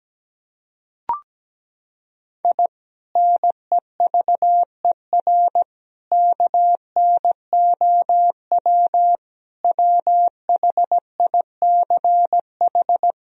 Morse code, heard as a keyed tone, about 17 wpm, 700 Hz. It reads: I NEVER KNOW WHICH
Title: Sentences from Top 1000 Words - Review Part 3 - 17wpm